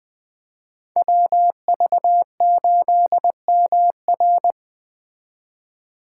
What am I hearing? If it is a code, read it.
Morse code, heard as a keyed tone, about 20 wpm, 700 Hz. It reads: WV8MR
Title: Speed Racing - USA 2x2 Call Signs 17wpm